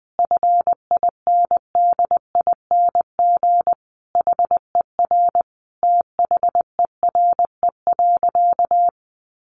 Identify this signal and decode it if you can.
Morse code, heard as a keyed tone, about 20 wpm, 700 Hz. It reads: FINDING HER THERE.
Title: Sentences from Top 1500 Words 20wpm